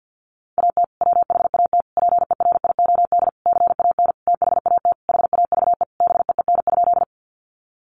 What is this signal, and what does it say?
Morse code, heard as a keyed tone, about 50 wpm, 700 Hz. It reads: AT WHAT PERIOD CAN THAT HAVE BEEN?
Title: Sentences from Top 800 Words 50wpm